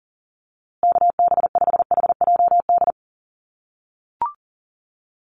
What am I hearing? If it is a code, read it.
Morse code, heard as a keyed tone, about 40 wpm, 700 Hz. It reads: KB5HJD
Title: USA Call Signs Repeated 2x 40wpm